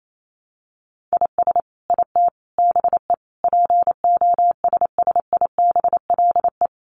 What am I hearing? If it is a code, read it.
Morse code, heard as a keyed tone, about 28 wpm, 700 Hz. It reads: IS IT BE POSSIBLE
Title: Sentences from Top 1200 Words 28wpm